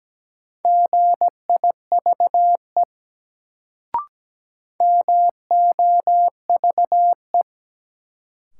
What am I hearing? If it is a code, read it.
Morse code, heard as a keyed tone, about 17 wpm, 700 Hz. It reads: GIVE MOVE
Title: Sentences from Top 200 Words - New Word Drill - 17wpm